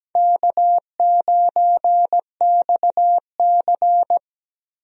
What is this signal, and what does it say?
Morse code, heard as a keyed tone, about 17 wpm, 700 Hz. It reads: K9XC
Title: USA Call Signs - 4 Characters - 17wpm